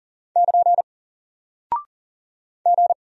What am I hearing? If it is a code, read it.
Morse code, heard as a keyed tone, about 40 wpm, 700 Hz. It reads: <KN> G
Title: Single Character (Letter:Number:Punctuation:Prosign) Alt 40wpm